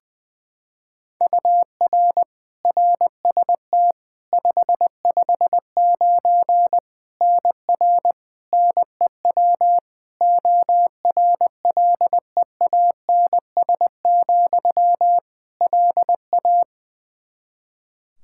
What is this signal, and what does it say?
Morse code, heard as a keyed tone, about 20 wpm, 700 Hz. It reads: UR RST 559 NR NEW ORLEANS, LA
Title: K1USN SST QSO Practice 20wpm